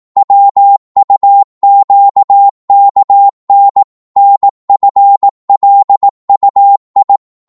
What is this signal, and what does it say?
Morse code, heard as a keyed tone, about 18 wpm, 800 Hz. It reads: WUQKN NFLUI
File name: Morse / ltr-18wpm